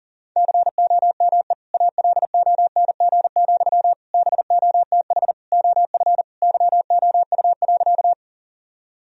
Morse code, heard as a keyed tone, about 40 wpm, 700 Hz. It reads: COME ALONG, BOTH OF YOU.
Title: Sentences from Top 1000 Words - Review Part 1 - 40wpm